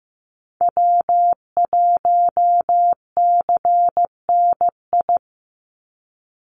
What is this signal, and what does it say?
Morse code, heard as a keyed tone, about 15 wpm, 700 Hz. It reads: W1CNI
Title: USA Call Signs 15wpm